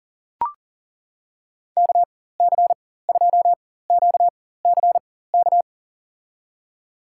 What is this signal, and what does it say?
Morse code, heard as a keyed tone, about 37 wpm, spaced as for 18 wpm, 700 Hz. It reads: KC2QCK
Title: USA Call Signs - 6 Characters - 4x Character Spacing - 40wpm